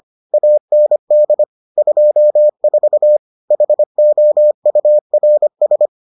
Morse code, heard as a keyed tone, about 25 wpm, 600 Hz. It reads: AND 24 HOURS